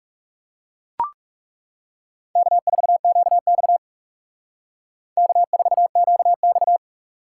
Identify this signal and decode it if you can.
Morse code, heard as a keyed tone, about 36 wpm, 700 Hz. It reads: K4QX K4QX
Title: Speed Racing - USA 1x2 Call Signs 30wpm